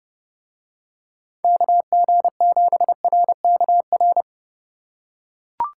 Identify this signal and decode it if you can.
Morse code, heard as a keyed tone, about 27 wpm, 700 Hz. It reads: KG7RKR